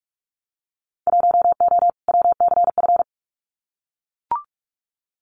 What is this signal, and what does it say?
Morse code, heard as a keyed tone, about 44 wpm, 700 Hz. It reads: 1O WCF
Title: Winter Field Day Exchanges 45wpm